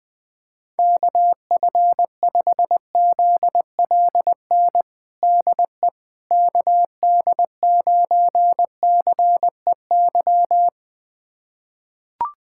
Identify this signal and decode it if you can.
Morse code, heard as a keyed tone, about 21 wpm, 700 Hz. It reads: KF5ZLN DE KD9CEY
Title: USA Callsign DE Callsign 20wpm